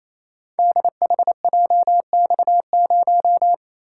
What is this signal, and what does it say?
Morse code, heard as a keyed tone, about 28 wpm, 700 Hz. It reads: DHJX0
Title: Code Groups - 5 Letter:Numbers - 28wpm